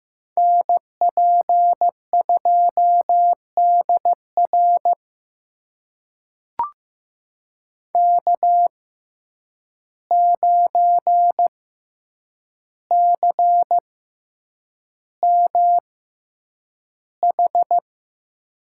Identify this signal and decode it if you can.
Morse code, heard as a keyed tone, about 15 wpm, spaced as for 8 wpm, 700 Hz. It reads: NP2DR K9CMH
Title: USA Call Signs - 5 Characters - 6x Character Spacing - 15wpm